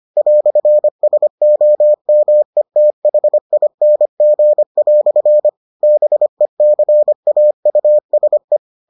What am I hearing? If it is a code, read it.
Morse code, heard as a keyed tone, about 25 wpm, 600 Hz. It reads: "SOMETHING" BECAUSE